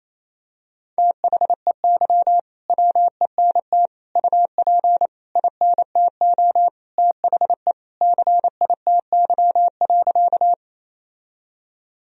Morse code, heard as a keyed tone, about 28 wpm, 700 Hz. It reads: THEY WENT UP INTO THE CITY.
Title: Sentences from Top 1000 Words - Review Part 1 - 28wpm